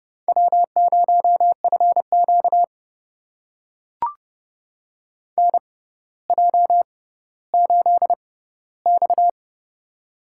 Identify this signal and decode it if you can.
Morse code, heard as a keyed tone, about 30 wpm, spaced as for 15 wpm, 700 Hz. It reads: W0FQ NJ8X
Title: USA Call Signs - 4 Characters - 6x Character Spacing - 30wpm